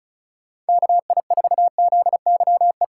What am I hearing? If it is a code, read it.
Morse code, heard as a keyed tone, about 36 wpm, 700 Hz. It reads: KI4ZYE
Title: USA Call Signs - 6 Characters - 3x Character Spacing - 35wpm